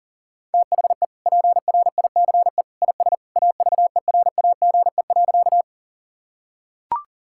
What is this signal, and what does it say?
Morse code, heard as a keyed tone, about 37 wpm, 700 Hz. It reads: THE PRICE IS AVERAGE.
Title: Sentences from Top 1000 Words - Review Part 6 - 40wpm